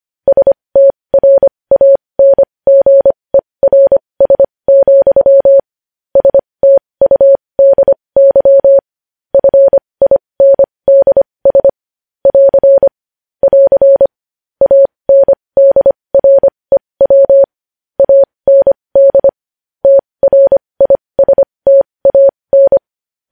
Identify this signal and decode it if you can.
Morse code, heard as a keyed tone, about 25 wpm, spaced as for 20 wpm, 550 Hz. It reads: STRANGERS, STUDY FINDS <AR> <AR> ANDREW AND TRISTAN
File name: podcast / News headlines-1x-25x20